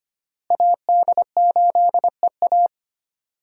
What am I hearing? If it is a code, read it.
Morse code, heard as a keyed tone, about 25 wpm, 700 Hz. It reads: AD8EA
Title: USA Call Signs - 5 Characters - Repeated 3x - 25wpm